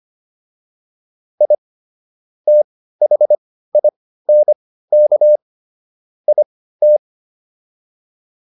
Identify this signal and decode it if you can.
Morse code, heard as a keyed tone, about 25 wpm, spaced as for 15 wpm, 600 Hz. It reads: I THINK IT
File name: WD205 15